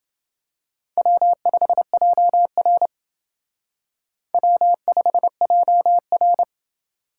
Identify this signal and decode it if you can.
Morse code, heard as a keyed tone, about 28 wpm, 700 Hz. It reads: W5JR W5JR